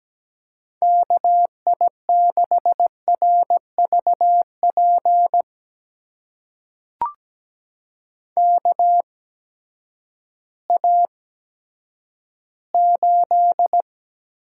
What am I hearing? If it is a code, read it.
Morse code, heard as a keyed tone, about 17 wpm, spaced as for 10 wpm, 700 Hz. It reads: KI6RVP KA8
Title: USA Call Signs - 6 Characters - 8x Character Spacing - 17wpm